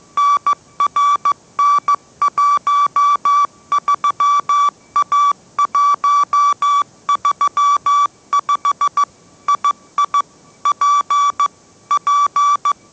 Morse code, heard as a keyed tone, about 15 wpm, 1150 Hz. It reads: NRN13A135 II P P
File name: mrx / mrx065 16wpm